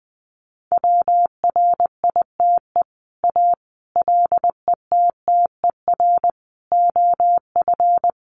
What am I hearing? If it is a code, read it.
Morse code, heard as a keyed tone, about 20 wpm, 700 Hz. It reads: WRITE A LETTER OF
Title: Sentences from Top 1400 Words 20wpm